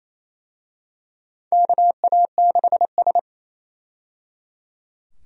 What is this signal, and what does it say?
Morse code, heard as a keyed tone, about 28 wpm, 700 Hz. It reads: KA6S
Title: USA Call Signs - 4 Characters - 28wpm